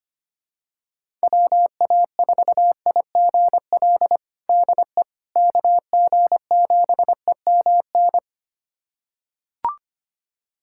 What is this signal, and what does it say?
Morse code, heard as a keyed tone, about 25 wpm, 700 Hz. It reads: WA4IGL DE KG7EMN
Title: USA Callsign DE Callsign - 6 Characters - 25wpm